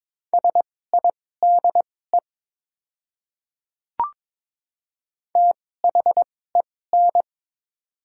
Morse code, heard as a keyed tone, about 21 wpm, spaced as for 16 wpm, 700 Hz. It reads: SIDE THEN
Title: 2x Character Spacing - Top 100 Words - 22wpm